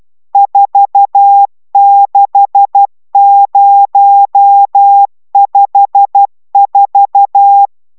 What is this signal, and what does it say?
Morse code, heard as a keyed tone, about 12 wpm, 800 Hz. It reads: 46054
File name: taviro tanulas / cw 69 20 10